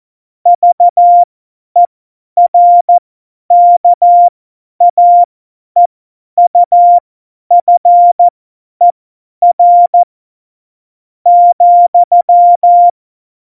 Morse code, heard as a keyed tone, about 14 wpm, spaced as for 9 wpm, 700 Hz. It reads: VERKAEUFER,
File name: Morsekurs / Witze / Witz 15. Januar 2019, 50 F70